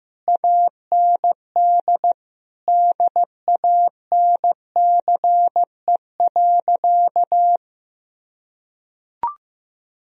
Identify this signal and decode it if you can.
Morse code, heard as a keyed tone, about 16 wpm, 700 Hz. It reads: AND DANCE.
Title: Sentences from Top 1000 Words - Review Part 4 - 15wpm